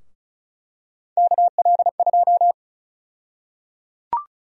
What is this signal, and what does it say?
Morse code, heard as a keyed tone, about 34 wpm, 700 Hz. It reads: KL2
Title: USA Call Sign Prefixes - 35wpm